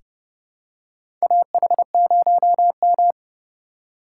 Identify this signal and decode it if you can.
Morse code, heard as a keyed tone, about 32 wpm, 700 Hz. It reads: AH0M
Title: USA Call Signs - 4 Characters - 3x Character Spacing - 30wpm